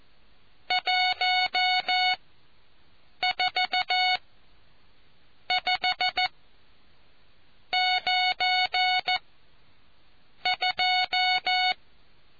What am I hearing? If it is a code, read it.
Morse code, heard as a keyed tone, about 14 wpm, spaced as for 6 wpm, 740 Hz. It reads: 145 92